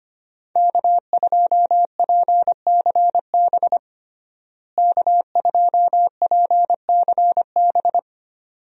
Morse code, heard as a keyed tone, about 25 wpm, 700 Hz. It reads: K2PCB K2PCB
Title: USA Call Signs - 5 Characters - Repeated 2x - 25wpm